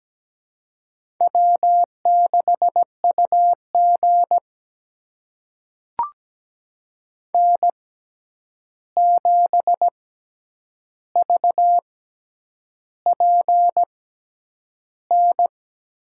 Morse code, heard as a keyed tone, about 17 wpm, spaced as for 8 wpm, 700 Hz. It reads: W6UG N7VPN
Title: USA Call Signs - 6x Character Spacing - 17wpm